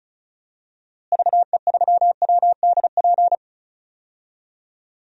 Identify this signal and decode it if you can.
Morse code, heard as a keyed tone, about 34 wpm, 700 Hz. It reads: VE3WDP
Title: DXCC-100 Call Signs - 6 Characters - Repeated 3x - 35wpm